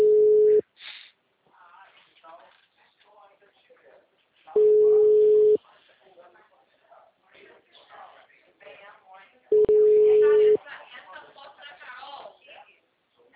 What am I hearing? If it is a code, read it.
Ringback tone (Europe, 425 Hz), with speech.
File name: out- -2003- - - .72099